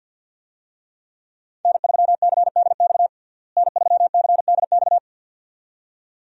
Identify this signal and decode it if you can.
Morse code, heard as a keyed tone, about 53 wpm, 700 Hz. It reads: N3CDX N3CDX
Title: USA Call Signs - 5 Characters - Repeated 2x - 50wpm